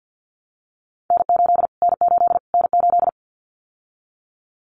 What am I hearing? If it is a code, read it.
Morse code, heard as a keyed tone, about 50 wpm, 700 Hz. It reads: N8 N8 N8
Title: Callsign Prefixes for Top 25 DXCC - Repeated 3x - 50wpm